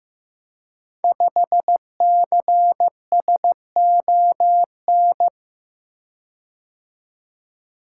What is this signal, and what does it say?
Morse code, heard as a keyed tone, about 15 wpm, 700 Hz. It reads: 5CSON
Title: Code Groups - 5 Letter:Numbers - 15wpm